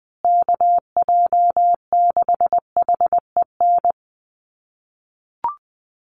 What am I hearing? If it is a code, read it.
Morse code, heard as a keyed tone, about 20 wpm, 700 Hz. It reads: KJ6HEN
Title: USA Call Signs Repeated 2x 20wpm